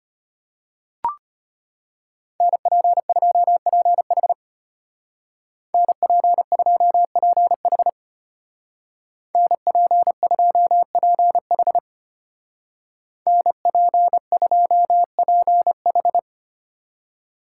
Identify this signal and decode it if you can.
Morse code, heard as a keyed tone, about 33 wpm, 700 Hz. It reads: NP2PH NP2PH NP2PH NP2PH